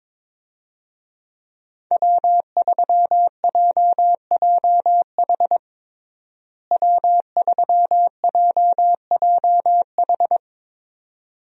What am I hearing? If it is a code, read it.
Morse code, heard as a keyed tone, about 22 wpm, 700 Hz. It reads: W3JJH W3JJH
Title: USA Call Signs - 5 Characters - Repeated 2x - 22wpm